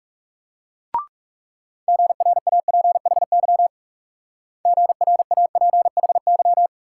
Morse code, heard as a keyed tone, about 37 wpm, 700 Hz. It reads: GRAPHY GRAPHY
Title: Speed-Racing Multi-letter Phonemes - 30wpm